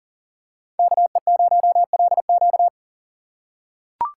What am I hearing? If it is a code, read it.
Morse code, heard as a keyed tone, about 40 wpm, 700 Hz. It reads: KE0LQ
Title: USA Call Signs - 4x Character Spacing - 40wpm